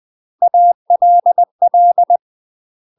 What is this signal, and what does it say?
Morse code, heard as a keyed tone, about 20 wpm, 700 Hz. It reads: ALL
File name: UK MASTER QSOs TRAINING / 20WPM / UK MASTER QSOs 0042